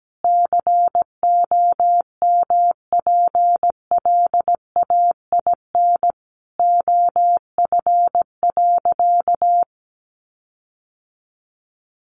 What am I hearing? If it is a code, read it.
Morse code, heard as a keyed tone, about 17 wpm, 700 Hz. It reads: COMPLAIN OF.
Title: Sentences from Top 1300 Words 17wpm